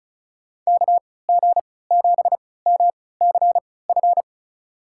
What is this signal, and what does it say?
Morse code, heard as a keyed tone, about 36 wpm, spaced as for 20 wpm, 700 Hz. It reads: KG7MCF
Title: USA Call Signs - 6 Characters - 3x Character Spacing - 35wpm